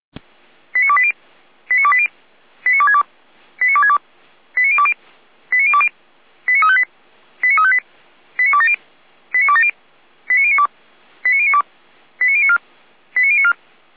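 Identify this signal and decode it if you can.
Five-tone selective call (ZVEI), heard as quick runs of five tones, tones of about 70 ms. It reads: 89280 89280 89262 89262 89020 89020 89358 89358 89270 89270 89092 89092 89094 89094
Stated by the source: ZVEI five-tone call received on an SDR.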